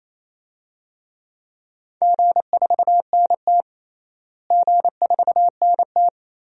Morse code, heard as a keyed tone, about 28 wpm, 700 Hz. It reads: G4NT G4NT
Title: DXCC-100 Call Signs - 4 Characters - Repeated 2x - 28wpm